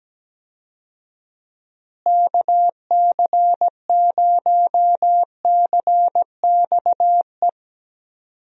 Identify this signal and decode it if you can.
Morse code, heard as a keyed tone, about 17 wpm, 700 Hz. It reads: KC0CXE
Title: USA Call Signs - 6 Characters - Repeated 2x - 17wpm